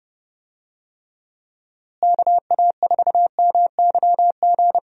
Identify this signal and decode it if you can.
Morse code, heard as a keyed tone, about 30 wpm, 700 Hz. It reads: KA4MYG
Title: USA Call Signs - 6 Characters - Repeated 3x - 30wpm